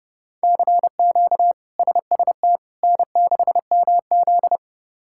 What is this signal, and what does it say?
Morse code, heard as a keyed tone, about 30 wpm, 700 Hz. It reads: CQ SST N6MZ
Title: K1USN SST QSO Practice - 30wpm